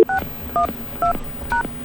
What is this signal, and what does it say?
DTMF, heard as telephone touch tones, tones of about 120 ms. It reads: 612#